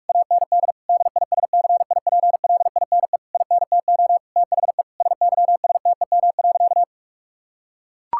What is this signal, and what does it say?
Morse code, heard as a keyed tone, about 41 wpm, 700 Hz. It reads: AND DISCIPLINE INTO THE SYSTEM.
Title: Sentences from Top 1000 Words - Review Part 6 - 45wpm